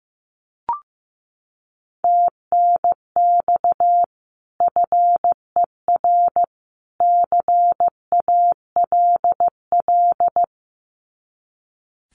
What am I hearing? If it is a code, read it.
Morse code, heard as a keyed tone, about 15 wpm, 700 Hz. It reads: TNX FER CALL
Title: Common QSO Phrases 15wpm